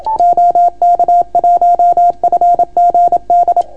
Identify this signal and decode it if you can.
Morse code, heard as a keyed tone, about 30 wpm, 690 Hz. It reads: OK1FGD